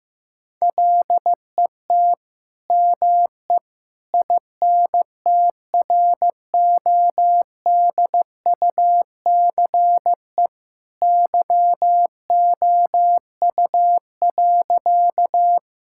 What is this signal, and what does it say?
Morse code, heard as a keyed tone, about 15 wpm, 700 Hz. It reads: LET ME INTRODUCE YOU.
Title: Sentences from Top 1600 Words 15wpm